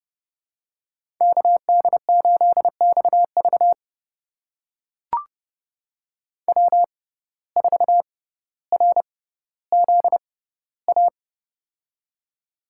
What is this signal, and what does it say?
Morse code, heard as a keyed tone, about 30 wpm, spaced as for 15 wpm, 700 Hz. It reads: KD8XV W4RZA
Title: USA Call Signs - 5 Characters - 6x Character Spacing - 30wpm